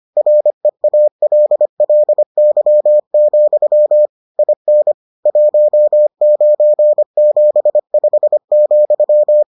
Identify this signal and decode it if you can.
Morse code, heard as a keyed tone, about 25 wpm, 600 Hz. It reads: REALLY, IN 1975,